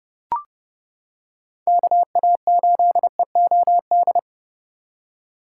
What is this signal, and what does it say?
Morse code, heard as a keyed tone, about 29 wpm, 700 Hz. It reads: KA8EOD